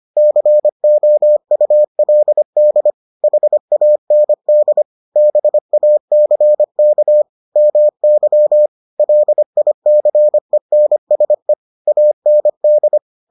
Morse code, heard as a keyed tone, about 25 wpm, 600 Hz. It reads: COULD HAND BACK MY LICENSE AND